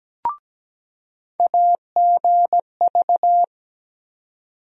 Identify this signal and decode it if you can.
Morse code, heard as a keyed tone, about 17 wpm, 700 Hz. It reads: AGV
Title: USA Call Sign Postfixes - 3 Letters 17wpm